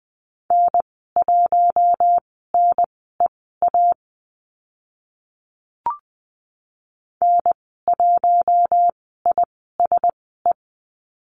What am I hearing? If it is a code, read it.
Morse code, heard as a keyed tone, about 20 wpm, spaced as for 14 wpm, 700 Hz. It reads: N1NEA N1ISE
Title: 042 Lesson 8g - Number 1 - Call Signs 20wpm